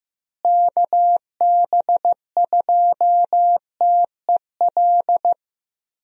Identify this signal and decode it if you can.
Morse code, heard as a keyed tone, about 14 wpm, 700 Hz. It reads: KB2TEL
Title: USA Call Signs - 6 Characters - Repeated 3x - 15wpm